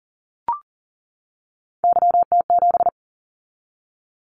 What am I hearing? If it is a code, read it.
Morse code, heard as a keyed tone, about 39 wpm, 700 Hz. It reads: YT7
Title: Callsign Prefixes for Top 25 DXCC - 40wpm